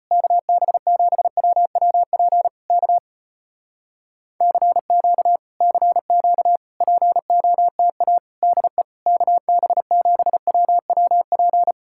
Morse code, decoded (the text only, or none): KB7WWP K CQ CQ POTA DE KB7WWP